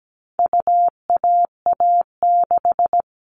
Morse code, heard as a keyed tone, about 17 wpm, 700 Hz. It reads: UAA6
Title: All Cut Numbers - 4-Digit 17wpm